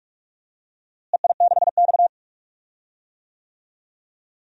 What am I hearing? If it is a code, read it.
Morse code, heard as a keyed tone, about 46 wpm, 700 Hz. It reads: EI6X
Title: DXCC-100 Call Signs - 4 Characters - Repeated 3x - 45wpm